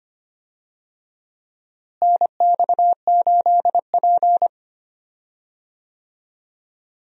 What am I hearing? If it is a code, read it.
Morse code, heard as a keyed tone, about 25 wpm, 700 Hz. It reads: NX8P
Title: USA Call Signs - 4 Characters - Repeated 2x - 25wpm